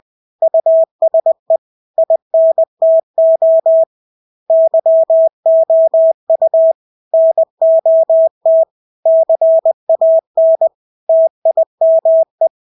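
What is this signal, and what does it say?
Morse code, heard as a keyed tone, about 20 wpm, 650 Hz. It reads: USE INTO YOU NOT CAN TIME
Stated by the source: Morse practice of English words at 20 wpm.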